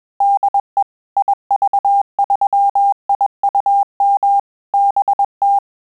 Morse code, heard as a keyed tone, about 21 wpm, 800 Hz. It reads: DE IV3IUM BT
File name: QSO CW 100 CPM